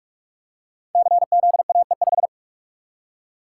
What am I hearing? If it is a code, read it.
Morse code, heard as a keyed tone, about 44 wpm, 700 Hz. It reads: CZAE5
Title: Code Groups - 5 Letter:Numbers - 45wpm